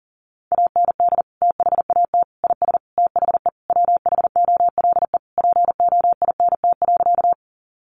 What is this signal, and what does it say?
Morse code, heard as a keyed tone, about 40 wpm, 700 Hz. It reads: AND THAT IS THE WHOLE POINT.
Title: Sentences from Top 400 Words 40wpm